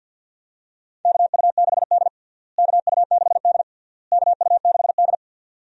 Morse code, heard as a keyed tone, about 51 wpm, 700 Hz. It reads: KU6D KU6D KU6D
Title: USA Call Signs - 4 Characters - Repeated 3x - 50wpm